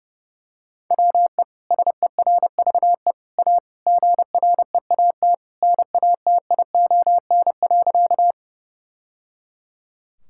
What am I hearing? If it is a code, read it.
Morse code, heard as a keyed tone, about 30 wpm, 700 Hz. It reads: WE SERVE A GREAT NATION.
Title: Sentences from Top 1000 Words - Review Part 6 - 30wpm